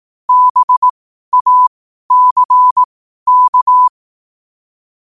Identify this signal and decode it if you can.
Morse code, heard as a keyed tone, about 18 wpm, spaced as for 12 wpm, 1000 Hz. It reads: BACK
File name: qotdfiles / 12wpm